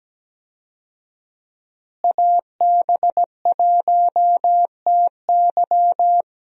Morse code, heard as a keyed tone, about 17 wpm, 700 Hz. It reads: AB1TY
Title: USA Call Signs - 5 Characters - Repeated 2x - 17wpm